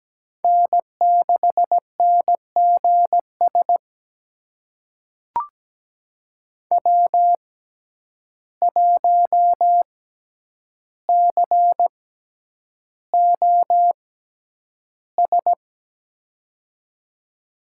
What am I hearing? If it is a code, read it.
Morse code, heard as a keyed tone, about 17 wpm, spaced as for 9 wpm, 700 Hz. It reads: N6NGS W1COS
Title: USA Call Signs - 5 Characters - 6x Character Spacing - 17wpm